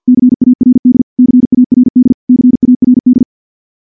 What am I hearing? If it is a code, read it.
Morse code, heard as a keyed tone, about 49 wpm, 250 Hz. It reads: CARD CARD CARD
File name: morse / morse-training-50WPM